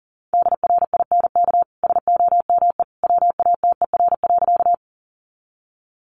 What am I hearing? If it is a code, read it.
Morse code, heard as a keyed tone, about 37 wpm, 700 Hz. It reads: DRINK SOME WATER.
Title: Sentences from Top 1500 Words 40wpm